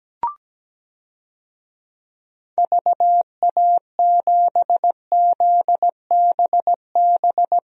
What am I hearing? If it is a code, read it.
Morse code, heard as a keyed tone, about 17 wpm, 700 Hz. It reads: VA7ZBB
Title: DXCC-100 Call Signs - 6 Characters - Repeated 2x - 17wpm